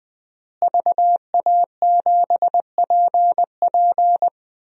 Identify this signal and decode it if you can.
Morse code, heard as a keyed tone, about 20 wpm, 700 Hz. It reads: VA7PP
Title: DXCC-100 Call Signs - 5 Characters - Repeated 3x - 20wpm